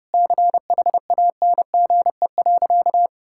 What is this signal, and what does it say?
Morse code, heard as a keyed tone, about 30 wpm, 700 Hz. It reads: CHANGE.
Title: Sentences from Top 1000 Words - Review Part 5 - 30wpm